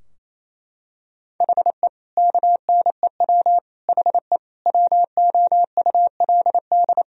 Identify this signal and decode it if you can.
Morse code, heard as a keyed tone, about 28 wpm, 700 Hz. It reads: HE KNEW HE WOULD
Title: Sentences from Top 1000 Words - Review Part 2 - 28wpm